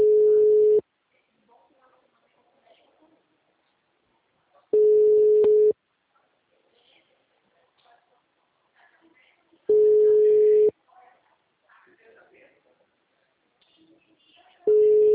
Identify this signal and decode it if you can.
Ringback tone (Europe, 425 Hz), with speech.